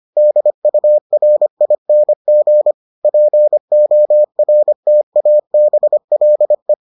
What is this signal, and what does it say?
Morse code, heard as a keyed tone, about 25 wpm, 600 Hz. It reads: DURING PORTABLE